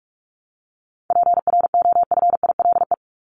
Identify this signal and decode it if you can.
Morse code, heard as a keyed tone, about 44 wpm, 700 Hz. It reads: PROFILE
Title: Top 1501-1750 Words 45wpm